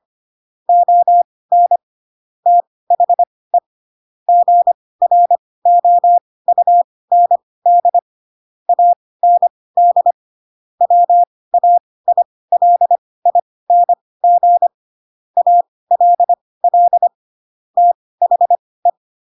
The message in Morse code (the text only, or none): ON THE GROUND AND WAILING ALL THE